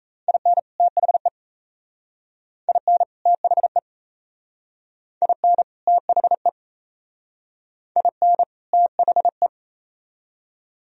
Morse code, heard as a keyed tone, about 35 wpm, 700 Hz. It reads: IN THE IN THE IN THE IN THE
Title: Speed-Racing - Sets of 2 Words 28wpm